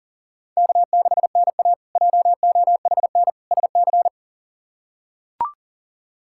Morse code, heard as a keyed tone, about 37 wpm, 700 Hz. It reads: K6NA JOHN SC
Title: K1USN SST QSO Practice - 40wpm